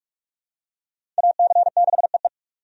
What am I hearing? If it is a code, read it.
Morse code, heard as a keyed tone, about 46 wpm, 700 Hz. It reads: AC6EE
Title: USA Call Signs - 5 Characters - 45wpm